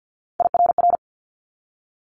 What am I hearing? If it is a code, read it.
Morse code, heard as a keyed tone, about 50 wpm, 690 Hz. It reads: IRR
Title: Multi-letter Phonemes 50wpm